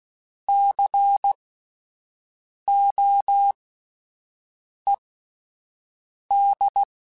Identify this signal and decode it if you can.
Morse code, heard as a keyed tone, about 16 wpm, spaced as for 5 wpm, 800 Hz. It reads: COED
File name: Alphabetical / Alphabetical mp3s / Cs @5